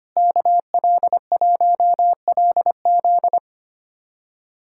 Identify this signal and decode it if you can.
Morse code, heard as a keyed tone, about 25 wpm, 700 Hz. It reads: KL1LZ
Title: USA Callsign DE Callsign - 5 Characters - 25wpm